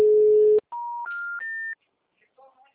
Special information tone.